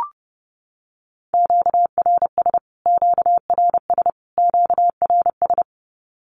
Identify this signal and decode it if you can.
Morse code, heard as a keyed tone, about 30 wpm, 700 Hz. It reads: QRS QRS QRS
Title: Q-Codes Repeated 3x 30wpm